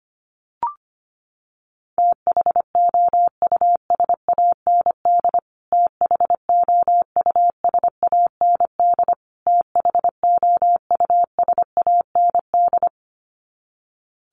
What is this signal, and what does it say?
Morse code, heard as a keyed tone, about 25 wpm, 700 Hz. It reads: THOUSAND THOUSAND THOUSAND
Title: Top 500 Words Repeated 25wpm